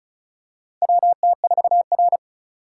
Morse code, heard as a keyed tone, about 36 wpm, 700 Hz. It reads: WT4R
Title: USA Call Signs - 4 Characters - 3x Character Spacing - 35wpm